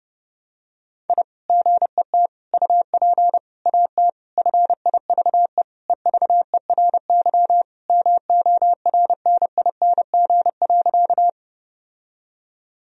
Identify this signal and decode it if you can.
Morse code, heard as a keyed tone, about 32 wpm, 700 Hz. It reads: I GET UP AT FIVE EVERY MORNING.
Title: Sentences from Top 1000 Words - Review Part 2 - 30wpm